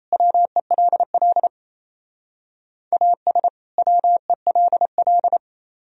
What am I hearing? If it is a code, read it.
Morse code, heard as a keyed tone, about 36 wpm, 700 Hz. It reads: WELL AS WELL